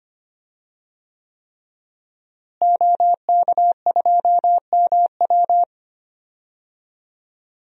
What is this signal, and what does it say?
Morse code, heard as a keyed tone, about 25 wpm, 700 Hz. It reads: OK2MW
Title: DXCC-100 Call Signs - 5 Characters - Repeated 3x - 25wpm